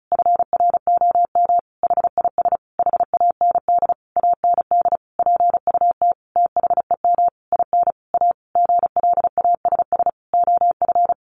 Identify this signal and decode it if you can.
Morse code, heard as a keyed tone, about 34 wpm, 700 Hz. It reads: FROM HIS HAND AND PUT THEM IN A GLASS OF
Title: Sentences from Top 1300 Words 35wpm